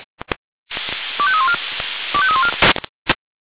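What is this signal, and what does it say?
Five-tone selective call (ZVEI), heard as quick runs of five tones, tones of about 70 ms. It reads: 25314 25314
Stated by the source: ZVEI five-tone selective call received by SDR.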